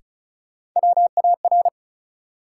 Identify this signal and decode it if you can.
Morse code, heard as a keyed tone, about 36 wpm, 700 Hz. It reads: WAR
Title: Sets of 5 Words - New Word Drill - 35wpm